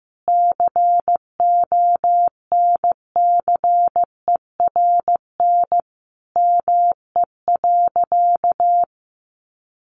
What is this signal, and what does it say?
Morse code, heard as a keyed tone, about 15 wpm, 700 Hz. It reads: CONCERN ME.
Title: Sentences from Top 1000 Words 15wpm